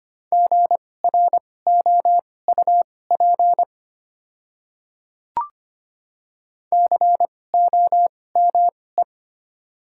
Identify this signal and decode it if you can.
Morse code, heard as a keyed tone, about 25 wpm, spaced as for 18 wpm, 700 Hz. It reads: GROUP COME
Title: 2x Character Spacing - Top 300 Words - 25wpm